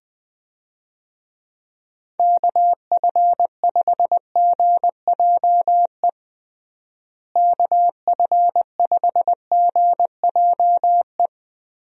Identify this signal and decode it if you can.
Morse code, heard as a keyed tone, about 21 wpm, 700 Hz. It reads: KF5GJE KF5GJE
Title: USA Call Signs - 6 Characters - Repeated 3x - 20wpm